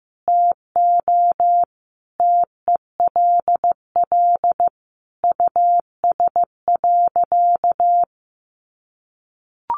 Morse code, heard as a keyed tone, about 15 wpm, 700 Hz. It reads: TO TELL US.
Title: Sentences from Top 1400 Words 15wpm